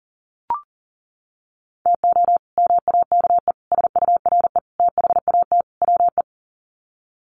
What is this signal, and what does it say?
Morse code, heard as a keyed tone, about 40 wpm, 700 Hz. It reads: TO MAKE SURE THAT WE
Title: Sets of 5 Words Encore 40wpm